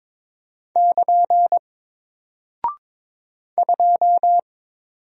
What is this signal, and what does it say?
Morse code, heard as a keyed tone, about 22 wpm, 700 Hz. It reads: <KN> 2
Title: Single Character (Letter:Number:Punctuation:Prosign) Alt 22wpm